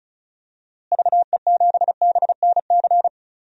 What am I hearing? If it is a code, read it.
Morse code, heard as a keyed tone, about 34 wpm, 700 Hz. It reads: VE7BNC
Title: DXCC-100 Call Signs - 6 Characters - Repeated 2x - 35wpm